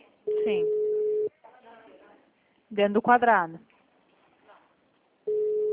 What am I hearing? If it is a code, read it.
Ringback tone (Europe, 425 Hz), with speech.
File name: out- -2006- - - .2269